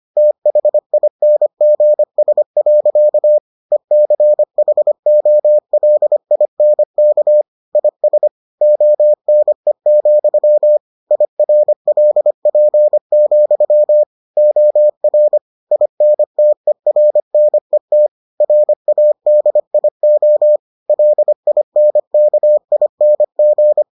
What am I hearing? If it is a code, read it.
Morse code, heard as a keyed tone, about 25 wpm, 600 Hz. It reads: THINGS. ECHOLINK IS ONE, IRLP, OR INTERNET RADIO LINKING